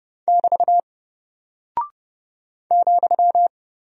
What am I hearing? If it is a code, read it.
Morse code, heard as a keyed tone, about 30 wpm, 700 Hz. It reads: <BT> ,
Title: Single Character (Letter:Number:Punctuation:Prosign) Alt 30wpm